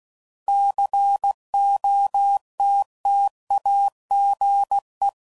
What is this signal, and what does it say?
Morse code, heard as a keyed tone, about 16 wpm, 800 Hz. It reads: COTTAGE
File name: CW Code Groups @15WPM / fromSM's / Cities